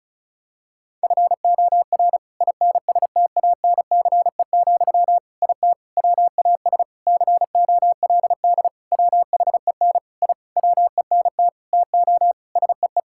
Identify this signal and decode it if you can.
Morse code, heard as a keyed tone, about 36 wpm, 700 Hz. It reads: FOR INSTANCE, IT WAS COLD WHEN I WENT TO SEE
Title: Sentences from Top 1000 Words - Review Part 5 - 35wpm